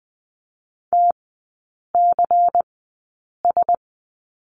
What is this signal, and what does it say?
Morse code, heard as a keyed tone, about 20 wpm, spaced as for 8 wpm, 690 Hz. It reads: TCS
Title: 093 Lesson 16e - Letter c - Three Characters 20wpm